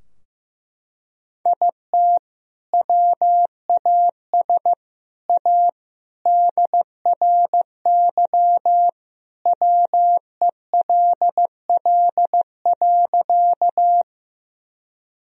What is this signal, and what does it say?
Morse code, heard as a keyed tone, about 15 wpm, 700 Hz. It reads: IT WAS A DRY WELL.
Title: Sentences from Top 1000 Words - Review Part 3 - 15wpm